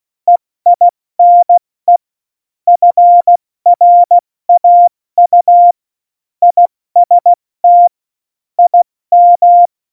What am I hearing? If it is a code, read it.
Morse code, heard as a keyed tone, about 16 wpm, 700 Hz. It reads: EINE FRAU IST IM
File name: Morsekurs / Witze / Witz 20. November 2018, 70 F80